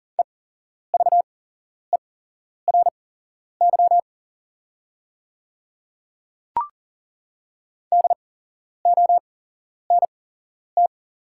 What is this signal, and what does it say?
Morse code, heard as a keyed tone, about 41 wpm, spaced as for 11 wpm, 690 Hz. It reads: EVERY DONT
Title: 8x Character Spacing - Top 500 Words - 40wpm